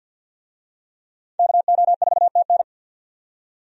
Morse code, heard as a keyed tone, about 52 wpm, 700 Hz. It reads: KO4TN
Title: USA Call Signs - 5 Characters - 50wpm